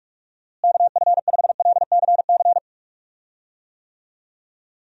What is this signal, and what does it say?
Morse code, heard as a keyed tone, about 46 wpm, 700 Hz. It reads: KF5LCC
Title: USA Call Signs - 6 Characters - Repeated 2x - 45wpm